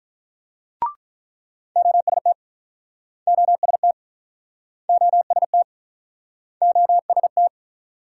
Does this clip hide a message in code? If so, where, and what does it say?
Yes: Morse code from 1.8 to 7.5 s, heard as a keyed tone, about 48 wpm, 700 Hz. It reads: OST OST OST OST